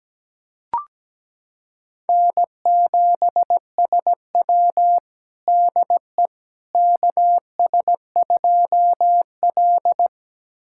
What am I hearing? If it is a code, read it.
Morse code, heard as a keyed tone, about 17 wpm, 700 Hz. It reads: N7SW DE KS2L
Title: USA Callsign DE Callsign - 4 Characters - 17wpm